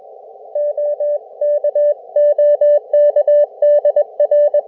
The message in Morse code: OK OK DR